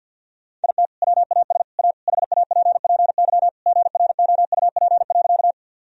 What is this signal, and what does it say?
Morse code, heard as a keyed tone, about 50 wpm, 700 Hz. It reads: IT WAS A HAPPY GROUP.
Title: Sentences from Top 1000 Words - Review Part 1 - 50wpm